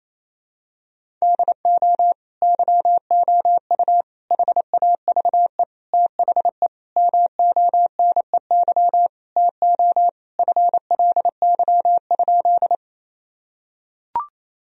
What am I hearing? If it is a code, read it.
Morse code, heard as a keyed tone, about 28 wpm, 700 Hz. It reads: DO YOU HAVE THE MONEY TO FLY?
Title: Sentences from Top 1000 Words - Review Part 2 - 28wpm